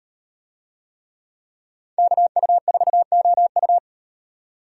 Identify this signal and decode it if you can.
Morse code, heard as a keyed tone, about 35 wpm, 700 Hz. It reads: KU4OU